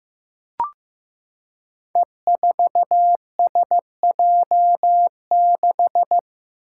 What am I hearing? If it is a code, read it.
Morse code, heard as a keyed tone, about 15 wpm, 700 Hz. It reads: E4SJ6
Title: Five Challenging Letter-Number Combinations 15wpm